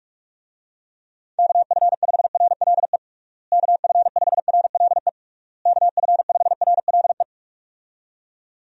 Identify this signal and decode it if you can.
Morse code, heard as a keyed tone, about 46 wpm, 700 Hz. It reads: KF5RLE KF5RLE KF5RLE
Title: USA Call Signs - 6 Characters - Repeated 3x - 45wpm